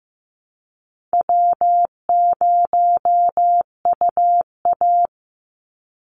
Morse code, heard as a keyed tone, about 15 wpm, 700 Hz. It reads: W0UA
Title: Easy Call Signs - 15wpm